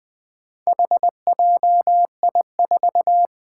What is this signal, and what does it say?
Morse code, heard as a keyed tone, about 20 wpm, 700 Hz. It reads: HJI4
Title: Four Challenging Letter-Number Combinations 20wpm